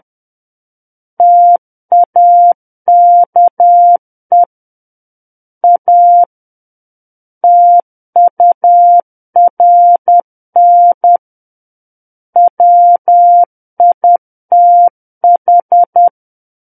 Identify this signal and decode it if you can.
Morse code, heard as a keyed tone, about 10 wpm, 700 Hz. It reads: TAKE A TURN WITH
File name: Book One / 01-Chapter 1 - 10 WPM